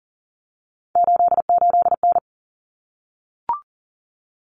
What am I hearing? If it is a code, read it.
Morse code, heard as a keyed tone, about 40 wpm, 700 Hz. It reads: 88N
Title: Common Cut Numbers - 3-Digit 40wpm